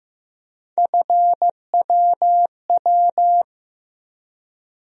Morse code, heard as a keyed tone, about 15 wpm, 700 Hz. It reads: FWW
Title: USA Call Sign Postfixes - 3 Letters 15wpm